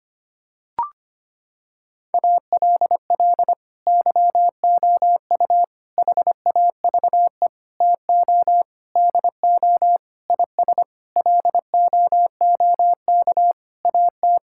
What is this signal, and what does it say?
Morse code, heard as a keyed tone, about 25 wpm, 700 Hz. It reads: ALL YOU HAVE TO DO IS LOOK AT
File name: Sets of 9 Words - 25wpm